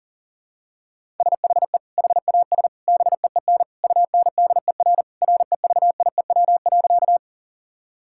Morse code, heard as a keyed tone, about 40 wpm, 700 Hz. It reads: SHE HAS BEEN UNDER REVIEW.
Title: Sentences from Top 1000 Words - Review Part 6 - 40wpm